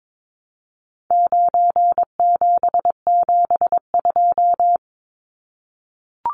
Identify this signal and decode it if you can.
Morse code, heard as a keyed tone, about 22 wpm, 700 Hz. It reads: 9772
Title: Numbers - 4-Digit 22wpm